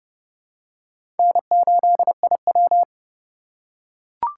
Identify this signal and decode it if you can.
Morse code, heard as a keyed tone, about 30 wpm, 700 Hz. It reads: N8IW
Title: USA Call Signs - 8x Character Spacing - 30wpm